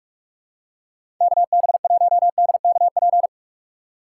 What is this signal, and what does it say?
Morse code, heard as a keyed tone, about 46 wpm, 700 Hz. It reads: KB1DKP
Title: USA Call Signs - 6 Characters - 4x Character Spacing - 45wpm